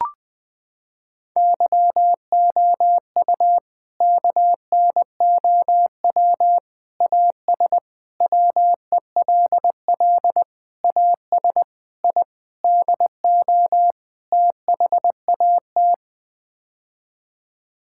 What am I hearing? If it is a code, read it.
Morse code, heard as a keyed tone, about 20 wpm, 700 Hz. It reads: YOU KNOW AS WELL AS I DO THAT
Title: Sets of 8 Words - 20wpm